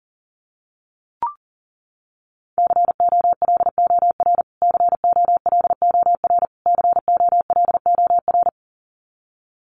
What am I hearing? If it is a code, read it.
Morse code, heard as a keyed tone, about 40 wpm, 700 Hz. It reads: COLOR COLOR COLOR
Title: Top 300 Words Repeated 40wpm